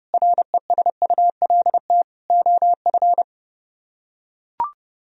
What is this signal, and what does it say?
Morse code, heard as a keyed tone, about 28 wpm, 700 Hz. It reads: RESULT OF